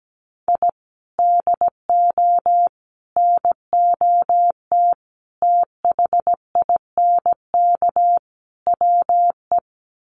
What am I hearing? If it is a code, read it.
Morse code, heard as a keyed tone, about 17 wpm, 700 Hz. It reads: I DO NOT THINK WE
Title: Sets of 5 Words 17wpm